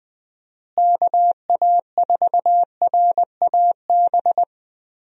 Morse code, heard as a keyed tone, about 20 wpm, 700 Hz. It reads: KA4RAB
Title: USA Call Signs - 6 Characters - Repeated 2x - 20wpm